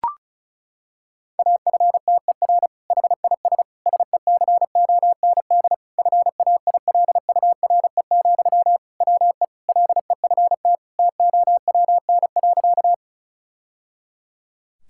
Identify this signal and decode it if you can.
Morse code, heard as a keyed tone, about 34 wpm, 700 Hz. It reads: AFTER HIS SECOND FAILURE, WE LEFT TOWN.
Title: Sentences from Top 1000 Words - Review Part 6 - 35wpm